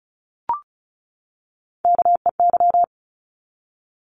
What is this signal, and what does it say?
Morse code, heard as a keyed tone, about 34 wpm, 700 Hz. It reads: KEY
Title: Top 501-750 Words 35wpm